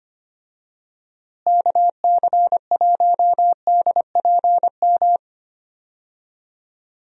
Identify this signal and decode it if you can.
Morse code, heard as a keyed tone, about 25 wpm, 700 Hz. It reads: KC1DPM
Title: USA Call Signs - 6 Characters - Repeated 3x - 25wpm